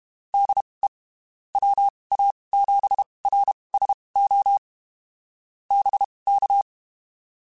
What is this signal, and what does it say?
Morse code, heard as a keyed tone, about 32 wpm, spaced as for 23 wpm, 800 Hz. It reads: DE WA7RSO BK
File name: (spaced) / QSOs Spaced-mp3s / WD0BGP 1 @30 sp